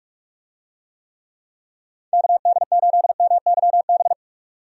Morse code, heard as a keyed tone, about 46 wpm, 700 Hz. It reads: KD8MYB
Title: USA Call Signs - 6 Characters - 45wpm